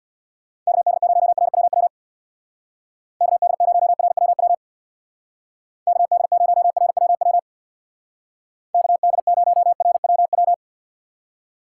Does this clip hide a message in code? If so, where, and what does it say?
Yes: Morse code from 0.7 to 10.5 s, heard as a keyed tone, about 64 wpm, 700 Hz. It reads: KD0RWW KD0RWW KD0RWW KD0RWW